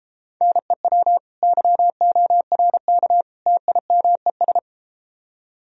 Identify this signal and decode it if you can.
Morse code, heard as a keyed tone, about 32 wpm, 700 Hz. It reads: NEW YORK TIMES